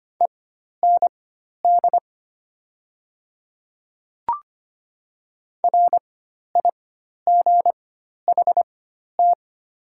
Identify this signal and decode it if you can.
Morse code, heard as a keyed tone, about 25 wpm, spaced as for 12 wpm, 690 Hz. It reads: END RIGHT
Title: 4x Character Spacing - Top 200 Words - 25wpm